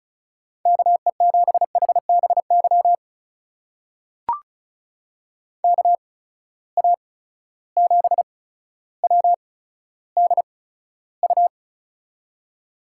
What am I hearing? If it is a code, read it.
Morse code, heard as a keyed tone, about 36 wpm, spaced as for 15 wpm, 700 Hz. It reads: KE7HBY KA7WDU
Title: USA Call Signs - 6 Characters - 8x Character Spacing - 35wpm